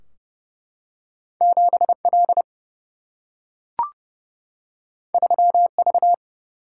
Morse code, heard as a keyed tone, about 29 wpm, 700 Hz. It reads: 7L 3V
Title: Single-Letter Single-Number 30wpm